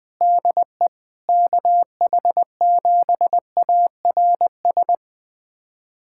Morse code, heard as a keyed tone, about 20 wpm, 700 Hz. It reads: DE KH7ARS
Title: USA Callsign DE Callsign - 6 Characters - 20wpm